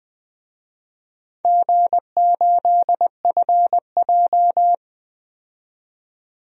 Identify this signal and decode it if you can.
Morse code, heard as a keyed tone, about 20 wpm, 700 Hz. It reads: G8FJ
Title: DXCC-100 Call Signs - 4 Characters - 20wpm